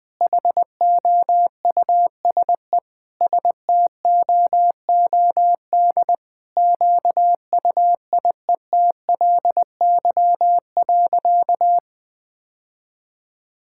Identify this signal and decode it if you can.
Morse code, heard as a keyed tone, about 20 wpm, 700 Hz. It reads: HOUSE STOOD QUIETLY.
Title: Sentences with 2-Word Phrases - 20wpm